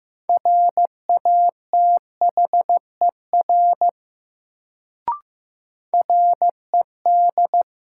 Morse code, heard as a keyed tone, about 15 wpm, 700 Hz. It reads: RATHER RED
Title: Sentences from Top 700 Words - New Word Drill - 15wpm